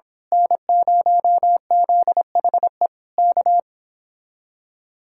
Morse code, heard as a keyed tone, about 26 wpm, 700 Hz. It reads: N0ZHE K
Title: Speed-Racing Common QSO Phrases - 22wpm